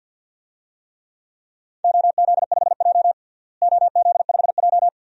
Morse code, heard as a keyed tone, about 50 wpm, 700 Hz. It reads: OZ5J OZ5J
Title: DXCC-100 Call Signs - 4 Characters - Repeated 2x - 50wpm